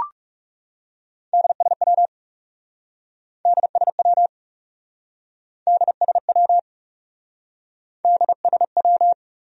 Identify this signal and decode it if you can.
Morse code, heard as a keyed tone, about 38 wpm, 700 Hz. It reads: DSW DSW DSW DSW